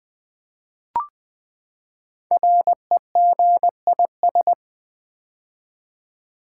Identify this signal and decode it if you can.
Morse code, heard as a keyed tone, about 20 wpm, 700 Hz. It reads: REGIS
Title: Build A Word - 5 Letters At A Time - 12-Letter Words - 20wpm